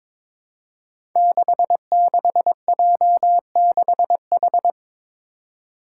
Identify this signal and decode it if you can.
Morse code, heard as a keyed tone, about 22 wpm, 700 Hz. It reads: 66J6H
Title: Five Challenging Letter-Number Combinations 22wpm